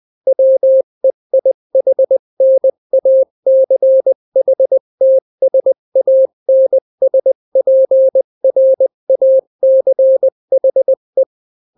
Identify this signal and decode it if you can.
Morse code, heard as a keyed tone, about 20 wpm, 530 Hz. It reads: WEIHNACHTSANSPRACHE
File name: daily / 18wpm